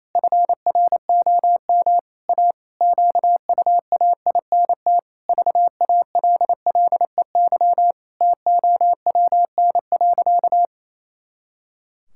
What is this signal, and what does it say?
Morse code, heard as a keyed tone, about 28 wpm, 700 Hz. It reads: FROM A QUAINT VALLEY TOWN.
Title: Pangram Challenge v2 - 28wpm